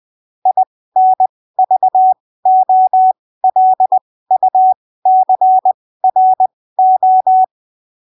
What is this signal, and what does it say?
Morse code, heard as a keyed tone, about 20 wpm, spaced as for 15 wpm, 750 Hz. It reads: INVOLUCRO
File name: arquivos / texto / texto 11 20ppm